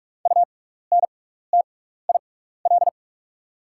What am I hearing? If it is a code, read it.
Morse code, heard as a keyed tone, about 47 wpm, spaced as for 16 wpm, 700 Hz. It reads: UNTIL
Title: 6x Character Spacing - Top 500 Words - 45wpm